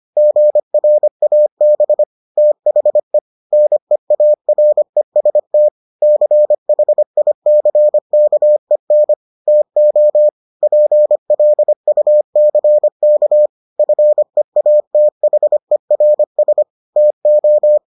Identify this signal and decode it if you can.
Morse code, heard as a keyed tone, about 25 wpm, 600 Hz. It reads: GRAB THE NEAREST CHICKEN TO PLUCK FEATHERS TO